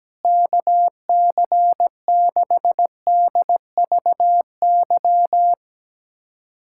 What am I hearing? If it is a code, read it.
Morse code, heard as a keyed tone, about 17 wpm, 700 Hz. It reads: KC6DVY
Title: USA Callsign DE Callsign 17wpm